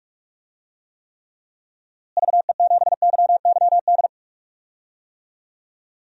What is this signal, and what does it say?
Morse code, heard as a keyed tone, about 42 wpm, 700 Hz. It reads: VE7YYD
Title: DXCC-100 Call Signs - 6 Characters - Repeated 2x - 45wpm